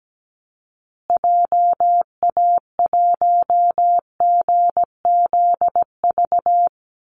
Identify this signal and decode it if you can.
Morse code, heard as a keyed tone, about 17 wpm, 700 Hz. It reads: JA1GZV
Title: Call Signs Repeated 2x - 17wpm